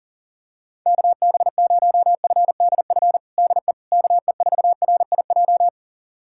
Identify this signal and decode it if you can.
Morse code, heard as a keyed tone, about 40 wpm, 700 Hz. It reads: KB0FDF DE KE4RIJ
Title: USA Callsign DE Callsign 40wpm